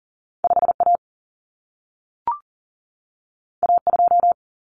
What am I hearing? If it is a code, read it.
Morse code, heard as a keyed tone, about 40 wpm, 700 Hz. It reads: 5A A2
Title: Common Cut Numbers - 2-Digit 40wpm